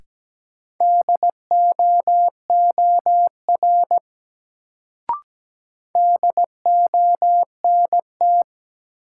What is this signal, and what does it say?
Morse code, heard as a keyed tone, about 17 wpm, 700 Hz. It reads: DOOR DONT
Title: Sentences from Top 300 Words - New Word Drill - 17wpm